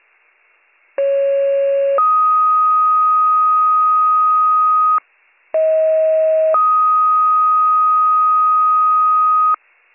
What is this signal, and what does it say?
Two-tone page, heard as a one-second tone, then a three-second tone, about 550 then 1250 Hz, about 640 then 1250 Hz. It is heard from 1.0 to 9.5 s.